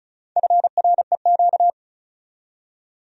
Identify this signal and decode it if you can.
Morse code, heard as a keyed tone, about 46 wpm, 700 Hz. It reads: FREQ